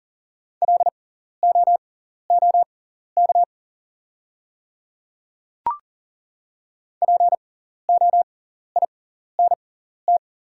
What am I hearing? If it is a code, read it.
Morse code, heard as a keyed tone, about 39 wpm, spaced as for 14 wpm, 700 Hz. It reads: LOOK POINT
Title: 6x Character Spacing - Top 200 Words - 40wpm